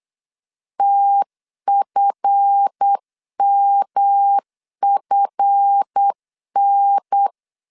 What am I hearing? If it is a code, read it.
Morse code, heard as a keyed tone, about 8 wpm, 800 Hz. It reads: TFMFN
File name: CW / DET / 1 9 1500C S A 8wpm 5637